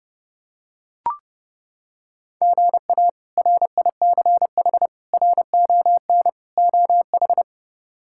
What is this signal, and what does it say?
Morse code, heard as a keyed tone, about 31 wpm, 700 Hz. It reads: GA RICH RON OH
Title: K1USN SST QSO Practice - 30wpm